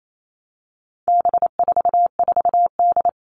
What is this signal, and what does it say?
Morse code, heard as a keyed tone, about 28 wpm, 700 Hz. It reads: B44D
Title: All Cut Numbers - 4-Digit 28wpm